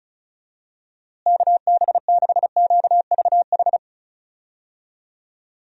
Morse code, heard as a keyed tone, about 36 wpm, 700 Hz. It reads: KB6QVH
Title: USA Call Signs - 6 Characters - Repeated 3x - 35wpm